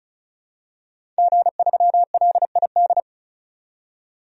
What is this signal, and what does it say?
Morse code, heard as a keyed tone, about 34 wpm, 700 Hz. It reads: G3LID
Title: DXCC-100 Call Signs - 5 Characters - 35wpm